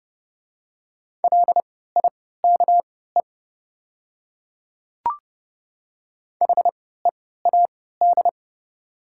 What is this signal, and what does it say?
Morse code, heard as a keyed tone, about 30 wpm, spaced as for 17 wpm, 700 Hz. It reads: LIKE HEAD
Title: 3x Character Spacing - Top 300 Words - 30wpm